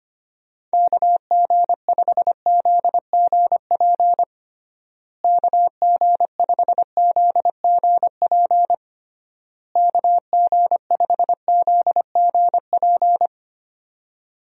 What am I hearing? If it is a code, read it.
Morse code, heard as a keyed tone, about 25 wpm, 700 Hz. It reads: KG5ZGP KG5ZGP KG5ZGP
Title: USA Call Signs - 6 Characters - Repeated 3x - 25wpm